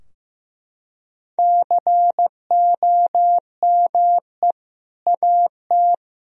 Morse code, heard as a keyed tone, about 15 wpm, 700 Hz. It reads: COME AT
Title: Sentences from Top 1000 Words - Review Part 2 - 15wpm